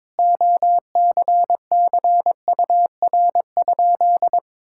Morse code, heard as a keyed tone, about 22 wpm, 700 Hz. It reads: OCCUR?
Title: Sentences from Top 1000 Words - Review Part 6 - 22wpm